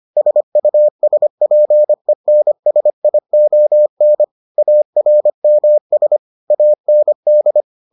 Morse code, heard as a keyed tone, about 25 wpm, 600 Hz. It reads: SUSPENSION ARMS AND